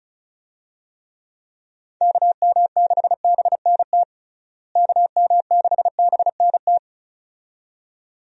Morse code, heard as a keyed tone, about 36 wpm, 700 Hz. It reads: KM6BNT KM6BNT
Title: USA Call Signs - 6 Characters - Repeated 2x - 35wpm